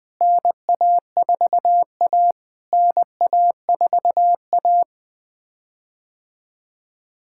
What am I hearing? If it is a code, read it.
Morse code, heard as a keyed tone, about 20 wpm, 700 Hz. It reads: NA4A NA4A
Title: POTA QSO Practice - 20wpm